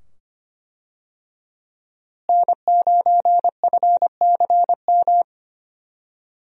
Morse code, heard as a keyed tone, about 25 wpm, 700 Hz. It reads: N9FCM
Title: USA Call Signs - 5 Characters - Repeated 2x - 25wpm